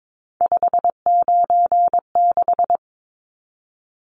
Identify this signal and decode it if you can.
Morse code, heard as a keyed tone, about 22 wpm, 700 Hz. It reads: 596
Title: Numbers - 3-Digit 22wpm